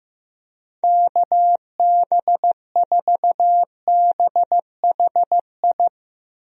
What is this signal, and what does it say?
Morse code, heard as a keyed tone, about 15 wpm, 700 Hz. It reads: KB4BHI
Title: USA Call Signs - 6 Characters - Repeated 2x - 15wpm